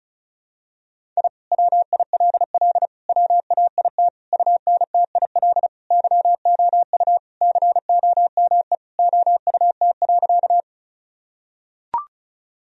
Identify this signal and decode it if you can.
Morse code, heard as a keyed tone, about 36 wpm, 700 Hz. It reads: I WILL WAIT UNTIL YOU COME OUT.
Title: Sentences from Top 1000 Words - Review Part 1 - 35wpm